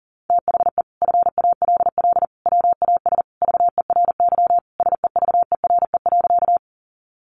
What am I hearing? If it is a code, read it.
Morse code, heard as a keyed tone, about 40 wpm, 700 Hz. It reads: THE FALL WAS VERY SEVERE.
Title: Sentences from Top 1400 Words 40wpm